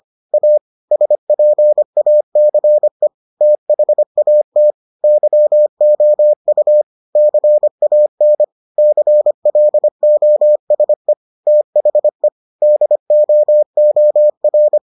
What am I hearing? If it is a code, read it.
Morse code, heard as a keyed tone, about 25 wpm, 600 Hz. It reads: A SPACE THAT YOU CAN CLOSE THE DOOR